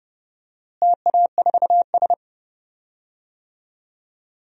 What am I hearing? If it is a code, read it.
Morse code, heard as a keyed tone, about 30 wpm, 700 Hz. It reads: TA4S
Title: DXCC-100 Call Signs - 4 Characters - Repeated 3x - 30wpm